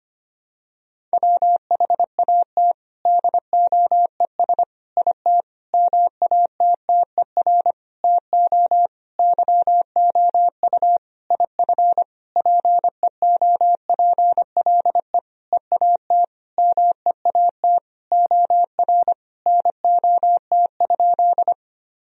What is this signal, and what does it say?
Morse code, heard as a keyed tone, about 25 wpm, 700 Hz. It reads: WHAT DOES IT MATTER TO YOU IF PEOPLE EAT MEAT OR NOT?
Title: Sentences from Top 1000 Words - Review Part 3 - 25wpm